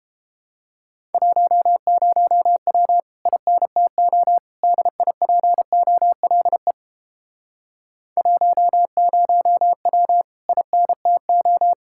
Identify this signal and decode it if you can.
Morse code, heard as a keyed tone, about 29 wpm, 700 Hz. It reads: 10W INTO DIPOLE 10W INTO